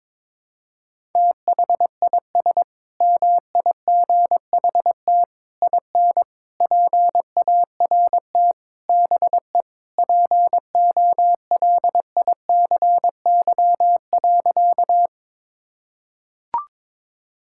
Morse code, heard as a keyed tone, about 21 wpm, 700 Hz. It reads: THIS MIGHT IN PART BE POLICY.
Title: Sentences from Top 1000 Words - Review Part 5 - 22wpm